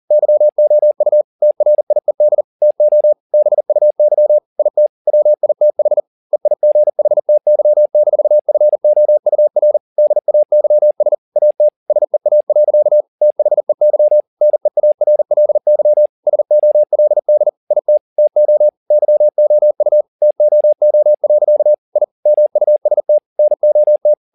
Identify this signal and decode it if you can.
Morse code, heard as a keyed tone, about 40 wpm, 600 Hz. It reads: YOU TRIED TO BUY IT WITH EIGHTY-FOUR DAYS AT SEA. THEY NEARLY SOLD IT TO YOU TOO. I MUST NOT
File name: the old man and the sea 2 40wpm